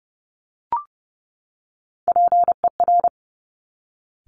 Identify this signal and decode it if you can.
Morse code, heard as a keyed tone, about 31 wpm, 700 Hz. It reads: PER
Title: Top Three Letter Combinations in Words 30wpm